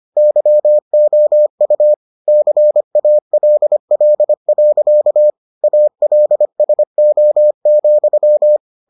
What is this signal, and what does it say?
Morse code, heard as a keyed tone, about 25 wpm, 600 Hz. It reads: YOU CALL. ALSO,